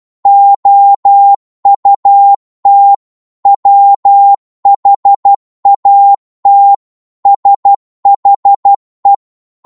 Morse code, heard as a keyed tone, about 12 wpm, 800 Hz. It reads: OUT WHAT SHE
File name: cwmp3 / psg-1-12wpm